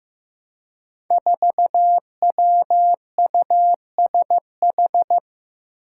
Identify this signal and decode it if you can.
Morse code, heard as a keyed tone, about 15 wpm, 700 Hz. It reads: 4WUSH
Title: Five Challenging Letter-Number Combinations 15wpm